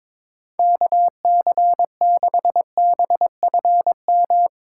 Morse code, heard as a keyed tone, about 22 wpm, 700 Hz. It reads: KC6BFM
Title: USA Call Signs - 6 Characters - 4x Character Spacing - 22wpm